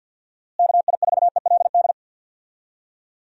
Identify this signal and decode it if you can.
Morse code, heard as a keyed tone, about 46 wpm, 700 Hz. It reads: KI4ELD
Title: USA Call Signs - 6 Characters - Repeated 3x - 50wpm